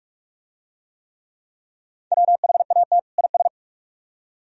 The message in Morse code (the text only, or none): WHAT IS